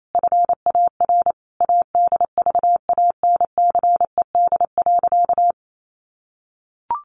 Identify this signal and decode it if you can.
Morse code, heard as a keyed tone, about 28 wpm, 700 Hz. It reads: FAR ADVANCED.
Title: Sentences from Top 1400 Words 28wpm